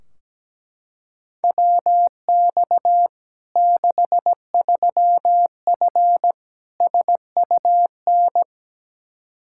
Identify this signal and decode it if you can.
Morse code, heard as a keyed tone, about 17 wpm, 700 Hz. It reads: WX 63F SUN
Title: Practice QSO - Exchange 2 - Info - 17wpm